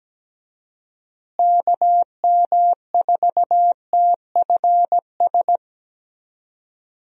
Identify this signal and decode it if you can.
Morse code, heard as a keyed tone, about 17 wpm, 700 Hz. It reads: KM4TFS
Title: USA Call Signs - 6 Characters - Repeated 3x - 17wpm